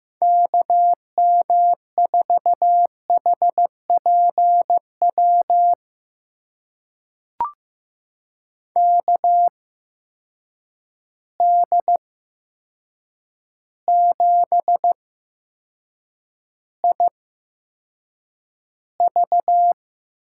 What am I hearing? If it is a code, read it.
Morse code, heard as a keyed tone, about 15 wpm, spaced as for 7 wpm, 700 Hz. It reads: KM4HPW KD7IV
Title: USA Call Signs - 6 Characters - 8x Character Spacing - 15wpm